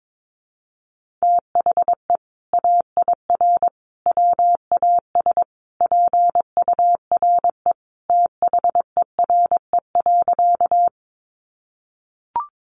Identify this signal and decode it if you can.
Morse code, heard as a keyed tone, about 22 wpm, 700 Hz. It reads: THE AIR WAS PURE THERE.
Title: Sentences from Top 1500 Words 22wpm